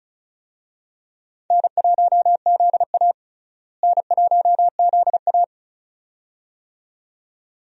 Morse code, heard as a keyed tone, about 36 wpm, 700 Hz. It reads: N1ZA N1ZA
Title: USA Call Signs - 4 Characters - Repeated 2x - 35wpm